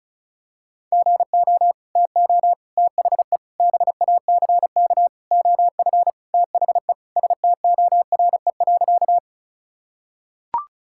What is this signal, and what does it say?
Morse code, heard as a keyed tone, about 36 wpm, 700 Hz. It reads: GO TO THE BACK OF THE STORE.
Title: Sentences from Top 1000 Words - Review Part 5 - 35wpm